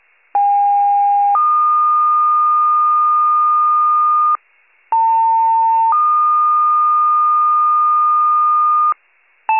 Two-tone page, heard as a one-second tone, then a three-second tone, about 810 then 1220 Hz, about 900 then 1220 Hz. It is heard from 0.3 to 8.9 s.
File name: wcalert